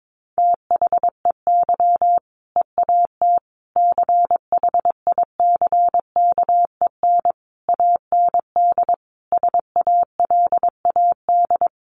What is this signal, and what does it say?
Morse code, heard as a keyed tone, about 22 wpm, 700 Hz. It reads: THEY EAT CHICKEN AND SALAD
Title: Easy Phrases 22wpm